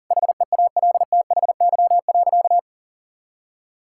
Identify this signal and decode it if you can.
Morse code, heard as a keyed tone, about 37 wpm, 700 Hz. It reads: HEALTHY.
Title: Sentences from Top 1000 Words - Review Part 5 - 40wpm